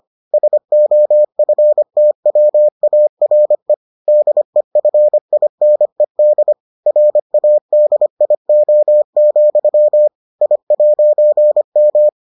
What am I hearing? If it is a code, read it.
Morse code, heard as a keyed tone, about 25 wpm, 600 Hz. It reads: SOFTWARE DEFINED RADIO, I'M